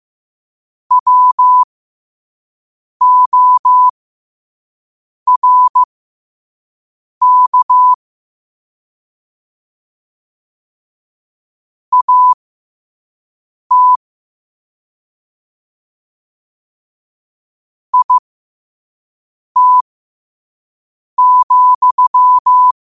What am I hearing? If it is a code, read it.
Morse code, heard as a keyed tone, about 15 wpm, spaced as for 5 wpm, 1000 Hz. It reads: WORK AT IT,
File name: qotdfiles / 5wpm